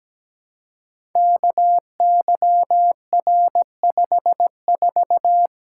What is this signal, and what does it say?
Morse code, heard as a keyed tone, about 17 wpm, 700 Hz. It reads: KYR54
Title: Code Groups - 5 Letter:Numbers - 17wpm